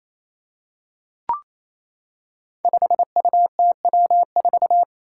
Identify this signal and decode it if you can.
Morse code, heard as a keyed tone, about 28 wpm, 700 Hz. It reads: 5UTW4
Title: Five Challenging Letter-Number Combinations 28wpm